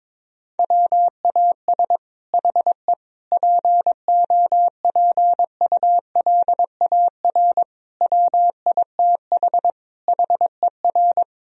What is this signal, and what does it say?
Morse code, heard as a keyed tone, about 22 wpm, 700 Hz. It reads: WAS HE POPULAR WITH HER
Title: Sentences from Top 1000 Words - Review Part 3 - 22wpm